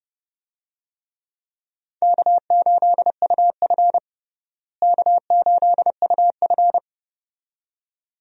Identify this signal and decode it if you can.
Morse code, heard as a keyed tone, about 30 wpm, 700 Hz. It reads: K8UF K8UF
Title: USA Call Signs - 4 Characters - Repeated 2x - 30wpm